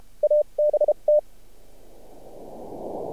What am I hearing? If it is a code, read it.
Morse code, heard as a keyed tone, about 32 wpm, spaced as for 27 wpm, 600 Hz, with radio noise. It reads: ABT
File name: archived / CRSnet